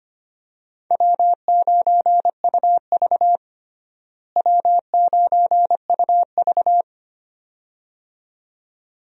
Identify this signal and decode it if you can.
Morse code, heard as a keyed tone, about 25 wpm, 700 Hz. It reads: W9UV W9UV
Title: USA Call Signs - 4 Characters - Repeated 2x - 25wpm